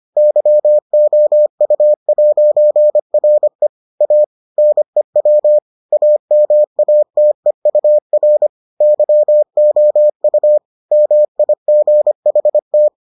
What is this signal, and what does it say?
Morse code, heard as a keyed tone, about 25 wpm, 600 Hz. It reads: YOU'RE A NEW AMATEUR YOU MIGHT